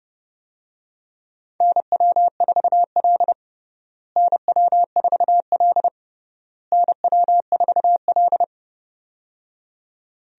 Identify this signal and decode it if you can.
Morse code, heard as a keyed tone, about 31 wpm, 700 Hz. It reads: NW4L NW4L NW4L
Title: USA Call Signs - 4 Characters - Repeated 3x - 30wpm